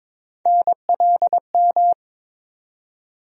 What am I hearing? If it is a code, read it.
Morse code, heard as a keyed tone, about 22 wpm, 700 Hz. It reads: NLM
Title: USA Call Sign Postfixes - 3 Letters 22wpm